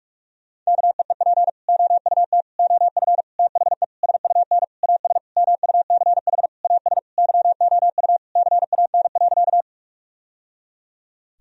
Morse code, heard as a keyed tone, about 46 wpm, 700 Hz. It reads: KEEP OUT OF THE SUN AS MUCH AS YOU CAN.
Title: Sentences from Top 1000 Words - Review Part 1 - 45wpm